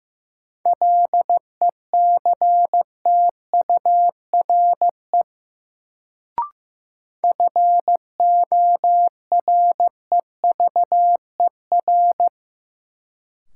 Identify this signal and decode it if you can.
Morse code, heard as a keyed tone, about 15 wpm, 700 Hz. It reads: LECTURE FOREVER
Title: Sentences from Top 1600 Words - New Word Drill Alt - 15wpm